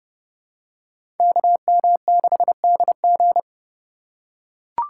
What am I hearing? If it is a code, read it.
Morse code, heard as a keyed tone, about 30 wpm, 700 Hz. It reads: KM6DG
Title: USA Call Signs - 5 Characters - 2x Character Spacing - 30wpm